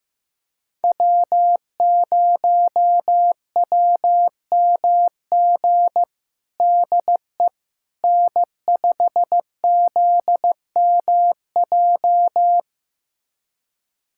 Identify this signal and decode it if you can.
Morse code, heard as a keyed tone, about 15 wpm, 700 Hz. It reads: W0WMG DE N5ZMJ
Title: USA Callsign DE Callsign - 5 Characters - 15wpm